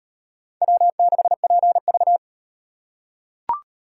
Morse code, heard as a keyed tone, about 38 wpm, 700 Hz. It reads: W6PV